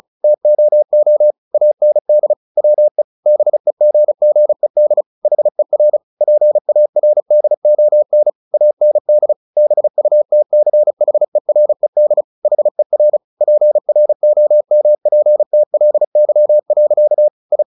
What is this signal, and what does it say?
Morse code, heard as a keyed tone, about 36 wpm, 600 Hz. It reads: TOO AND WE BEGGED HER PARDON AND BUTCHERED HER PROMPTLY. I
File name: the old man and the sea / the old man and the sea 1 35wpm